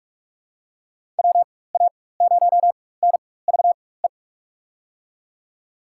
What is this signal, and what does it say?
Morse code, heard as a keyed tone, about 46 wpm, spaced as for 21 wpm, 700 Hz. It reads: WA0NVE
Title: USA Call Signs - 6 Characters - 4x Character Spacing - 45wpm